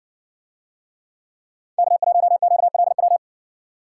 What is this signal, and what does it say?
Morse code, heard as a keyed tone, about 56 wpm, 700 Hz. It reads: K1QLW